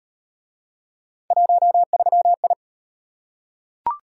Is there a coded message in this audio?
Yes: Morse code from 1.3 to 2.5 s, heard as a keyed tone, about 38 wpm, 700 Hz. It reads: 13I